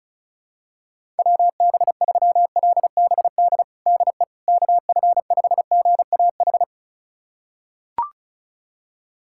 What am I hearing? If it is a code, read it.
Morse code, heard as a keyed tone, about 34 wpm, 700 Hz. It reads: WB3LBD DE KF5GAH
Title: USA Callsign DE Callsign 35wpm